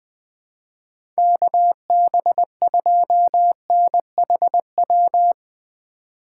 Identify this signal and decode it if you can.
Morse code, heard as a keyed tone, about 20 wpm, 700 Hz. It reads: KB2NHW
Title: USA Call Signs - 6 Characters - 6x Character Spacing - 20wpm